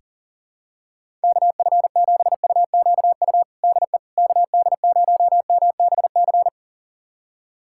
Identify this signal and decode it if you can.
Morse code, heard as a keyed tone, about 40 wpm, 700 Hz. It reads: KF7UQU DE KD0MBC
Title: USA Callsign DE Callsign - 6 Characters - 40wpm